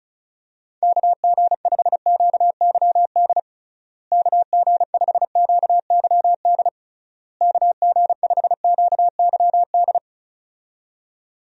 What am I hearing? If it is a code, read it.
Morse code, heard as a keyed tone, about 34 wpm, 700 Hz. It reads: KG5QYD KG5QYD KG5QYD
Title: USA Call Signs - 6 Characters - Repeated 3x - 35wpm